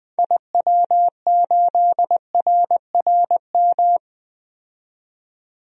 Morse code, heard as a keyed tone, about 20 wpm, 700 Hz. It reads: IW8RRM
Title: DXCC-100 Call Signs - 6 Characters - 20wpm